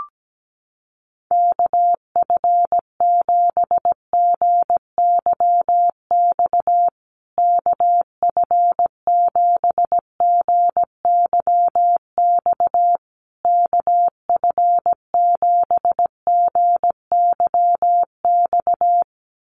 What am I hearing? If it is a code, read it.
Morse code, heard as a keyed tone, about 17 wpm, 700 Hz. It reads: KF7GYX KF7GYX KF7GYX
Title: USA Call Signs Repeated 3x 17wpm